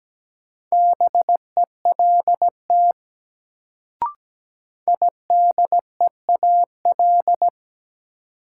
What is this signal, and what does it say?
Morse code, heard as a keyed tone, about 17 wpm, 700 Hz. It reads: BELT IDEAL
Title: Sentences from Top 1400 Words - New Word Drill Alt - 17wpm